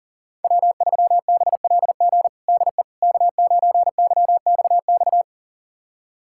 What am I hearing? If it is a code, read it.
Morse code, heard as a keyed tone, about 40 wpm, 700 Hz. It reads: W3BLG DE K9YXX
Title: USA Callsign DE Callsign - 5 Characters - 40wpm